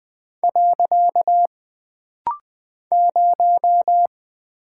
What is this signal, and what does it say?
Morse code, heard as a keyed tone, about 20 wpm, 700 Hz. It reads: . 0
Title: Single Character (Letter:Number:Punctuation:Prosign) - Rapid-Fire - 20wpm